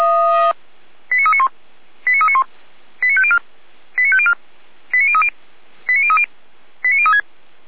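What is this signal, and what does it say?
Five-tone selective call (ZVEI), heard as quick runs of five tones, tones of about 70 ms. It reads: 89391 89391 89504 89504 89030 89030 89036
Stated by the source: ZVEI five-tone selective call received on an SDR.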